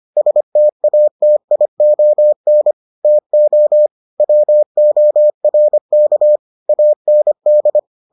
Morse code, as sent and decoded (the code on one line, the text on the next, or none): ... - .- - .. --- -. / - --- / .-- --- .-. -.- / .- -. -..
STATION TO WORK AND